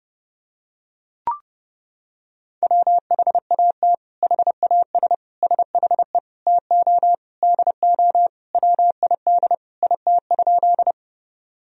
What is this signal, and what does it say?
Morse code, heard as a keyed tone, about 30 wpm, 700 Hz. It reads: WHAT HAS SHE TO DO WID IT?
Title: Sentences from Top 100 Words and Abbreviations 30wpm